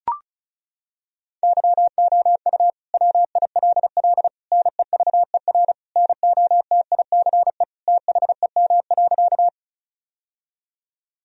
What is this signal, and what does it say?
Morse code, heard as a keyed tone, about 36 wpm, 700 Hz. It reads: YOU WILL NEVER NOTICE THEM.
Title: Sentences from Top 1000 Words - Review Part 2 - 35wpm